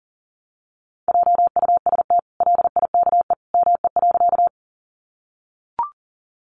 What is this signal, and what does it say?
Morse code, heard as a keyed tone, about 40 wpm, 700 Hz. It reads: JUST LIKE ME.
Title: Sentences from Top 200 Words 40wpm